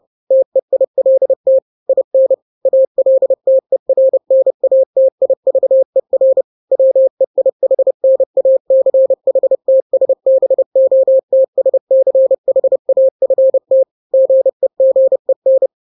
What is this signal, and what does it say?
Morse code, heard as a keyed tone, about 30 wpm, 530 Hz. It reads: TEILT IN ALTERNATIVER WEIHNACHTSBOTSCHAFT GEGEN
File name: daily / 29wpm